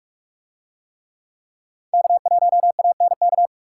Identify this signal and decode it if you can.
Morse code, heard as a keyed tone, about 44 wpm, 700 Hz. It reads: K1ANK